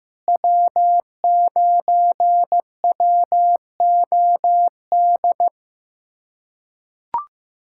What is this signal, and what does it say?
Morse code, heard as a keyed tone, about 15 wpm, 700 Hz. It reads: W9WOD
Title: USA Callsign DE Callsign - 5 Characters - 15wpm